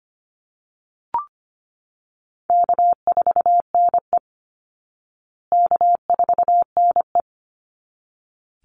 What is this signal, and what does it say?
Morse code, heard as a keyed tone, about 25 wpm, 700 Hz. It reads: K4NE K4NE
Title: Call Signs Repeated 3x - 25wpm